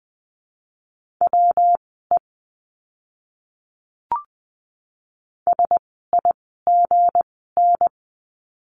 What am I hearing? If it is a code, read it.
Morse code, heard as a keyed tone, about 20 wpm, spaced as for 15 wpm, 700 Hz. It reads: WE SIGN
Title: 160 Lesson 26f - Letter g - Words 20wpm